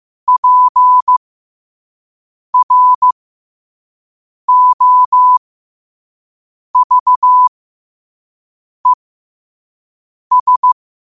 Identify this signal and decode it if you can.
Morse code, heard as a keyed tone, about 15 wpm, spaced as for 5 wpm, 1000 Hz. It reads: PROVES
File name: qotdfiles / 5wpm